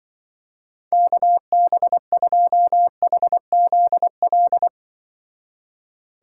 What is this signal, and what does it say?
Morse code, heard as a keyed tone, about 24 wpm, 700 Hz. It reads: KB2HZL
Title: Speed Racing - USA Call Signs 20wpm